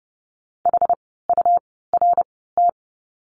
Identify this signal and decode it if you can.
Morse code, heard as a keyed tone, about 30 wpm, spaced as for 17 wpm, 700 Hz. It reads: HURT
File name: Lesson 17f - Letter u - Words 30wpm